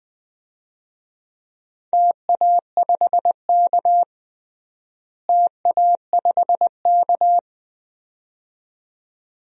Morse code, heard as a keyed tone, about 21 wpm, 700 Hz. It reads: TA5K TA5K
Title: DXCC-100 Call Signs - 4 Characters - Repeated 2x - 20wpm